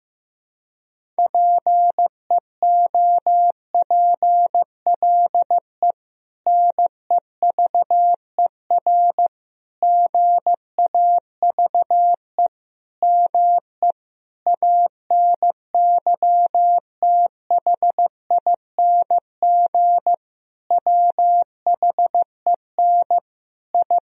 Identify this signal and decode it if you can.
Morse code, heard as a keyed tone, about 16 wpm, 700 Hz. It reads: PEOPLE NEVER GAVE ME ANYTHING WHEN I
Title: Sentences from Top 1000 Words - Review Part 1 - 15wpm